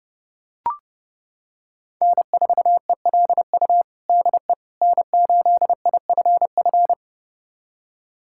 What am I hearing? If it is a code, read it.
Morse code, heard as a keyed tone, about 30 wpm, 700 Hz. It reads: N4ELU DE N8IFF
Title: USA Callsign DE Callsign - 5 Characters - 30wpm